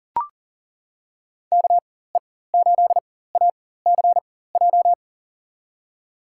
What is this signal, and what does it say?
Morse code, heard as a keyed tone, about 40 wpm, spaced as for 19 wpm, 700 Hz. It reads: KE8ACJ
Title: USA Call Signs - 6 Characters - 4x Character Spacing - 40wpm